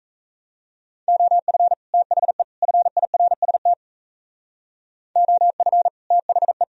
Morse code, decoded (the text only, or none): OF THE FIRST OF THE